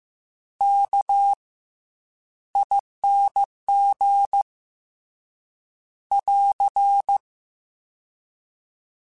Morse code, heard as a keyed tone, about 15 wpm, spaced as for 10 wpm, 800 Hz. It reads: KING <AR>
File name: Special INGs-mp3s / Special INGs-mp3s / Fast ing @13